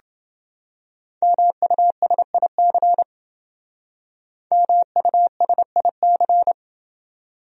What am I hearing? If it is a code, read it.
Morse code, heard as a keyed tone, about 28 wpm, 700 Hz. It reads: MUSIC MUSIC